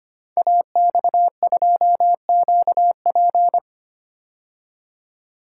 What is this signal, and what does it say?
Morse code, heard as a keyed tone, about 25 wpm, 700 Hz. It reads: AX2QP
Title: Code Groups - 5 Letter:Numbers - 25wpm